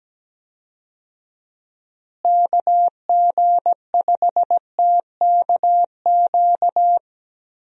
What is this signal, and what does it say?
Morse code, heard as a keyed tone, about 17 wpm, 700 Hz. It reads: KG5TKQ
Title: USA Call Signs - 6 Characters - Repeated 2x - 17wpm